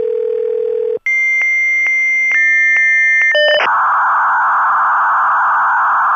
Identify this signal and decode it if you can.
Dial-up modem handshake.
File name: Modem connection, signal, interference